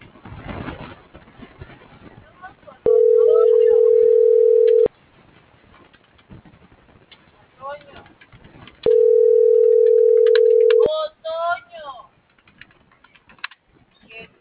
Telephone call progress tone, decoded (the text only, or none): ringback tone